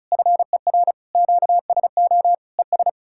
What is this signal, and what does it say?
Morse code, heard as a keyed tone, about 47 wpm, 700 Hz. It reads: FER QSO ES